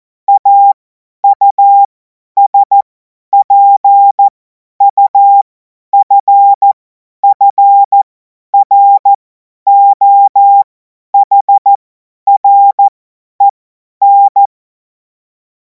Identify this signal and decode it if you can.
Morse code, heard as a keyed tone, about 14 wpm, spaced as for 10 wpm, 800 Hz. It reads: AUSPUFFROHREN
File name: Morsekurs / Witze / Witz 23. Juni 2026 50 F70